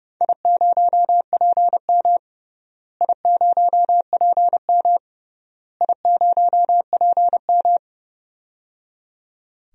Morse code, heard as a keyed tone, about 29 wpm, 700 Hz. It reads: I0PM I0PM I0PM
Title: DXCC-100 Call Signs - 4 Characters - Repeated 3x - 30wpm